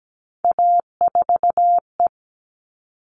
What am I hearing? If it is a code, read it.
Morse code, heard as a keyed tone, about 17 wpm, 700 Hz. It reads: A4E
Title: All Cut Numbers - 3-Digit 17wpm